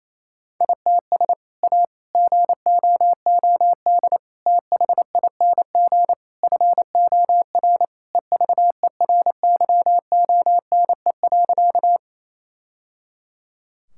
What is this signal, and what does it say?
Morse code, heard as a keyed tone, about 28 wpm, 700 Hz. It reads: ITS A GOOD THING FOR EVERYONE.
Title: Sentences from Top 1000 Words - Review Part 5 - 28wpm